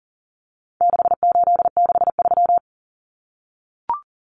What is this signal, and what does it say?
Morse code, heard as a keyed tone, about 40 wpm, 700 Hz. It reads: 6863
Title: Common Cut Numbers - 4-Digit 40wpm